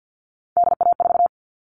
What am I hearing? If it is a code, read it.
Morse code, heard as a keyed tone, about 50 wpm, 700 Hz. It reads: DA4
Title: All Cut Numbers - 3-Digit 50wpm